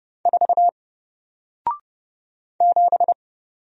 Morse code, heard as a keyed tone, about 30 wpm, 700 Hz. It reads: 4 7
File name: Single-Letter-Number-30wpm